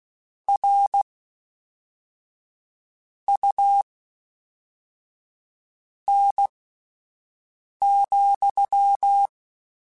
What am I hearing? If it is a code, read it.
Morse code, heard as a keyed tone, about 16 wpm, spaced as for 4 wpm, 800 Hz. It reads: RUN,
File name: Misc / Misc-mp3s / WordPairs sp @5